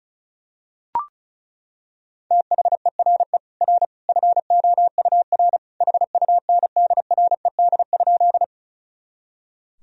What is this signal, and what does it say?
Morse code, heard as a keyed tone, about 36 wpm, 700 Hz. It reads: THERE R FOUR HUNDRED?
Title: Sentences from Top 400 Words and Abbreviations 35wpm